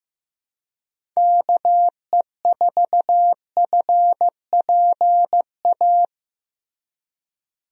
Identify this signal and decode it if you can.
Morse code, heard as a keyed tone, about 15 wpm, 700 Hz. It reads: KE4FPA
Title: USA Call Signs - 6 Characters - Repeated 3x - 15wpm